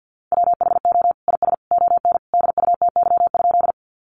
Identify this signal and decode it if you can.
Morse code, heard as a keyed tone, about 50 wpm, 700 Hz. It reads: WHO IS ON DUTY?
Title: Sentences from Top 1500 Words 50wpm